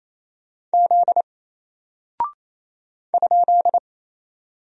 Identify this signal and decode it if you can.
Morse code, heard as a keyed tone, about 28 wpm, 700 Hz. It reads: Z ?
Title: Single Character (Letter:Number:Punctuation:Prosign) 28wpm